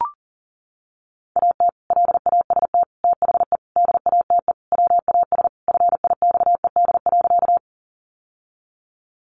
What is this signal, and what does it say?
Morse code, heard as a keyed tone, about 40 wpm, 700 Hz. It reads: AT LAST THE DATE WAS FIXED.
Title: Sentences from Top 1100 Words 40wpm